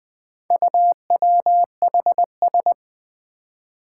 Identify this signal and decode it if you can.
Morse code, heard as a keyed tone, about 20 wpm, 700 Hz. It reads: UWHS
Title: Four Challenging Letter-Number Combinations 20wpm